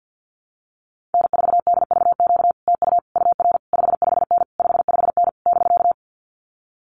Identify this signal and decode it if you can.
Morse code, heard as a keyed tone, about 50 wpm, 700 Hz. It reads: N4DVQ TU UR 55N 55N <BK>
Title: Common QSO Phrases Encore 50wpm